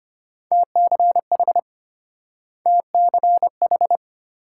Morse code, heard as a keyed tone, about 33 wpm, 700 Hz. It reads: TCH TCH